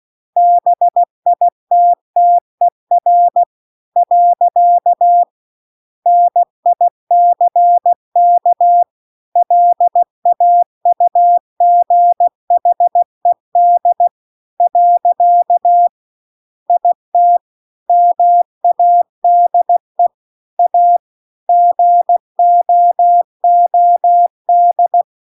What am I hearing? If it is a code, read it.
Morse code, heard as a keyed tone, about 16 wpm, 700 Hz. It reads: BITTER . NICK LAUGHED . IT MADE A GOOD